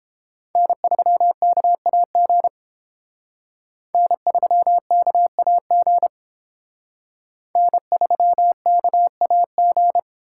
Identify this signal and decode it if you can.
Morse code, heard as a keyed tone, about 28 wpm, 700 Hz. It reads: N3KAG N3KAG N3KAG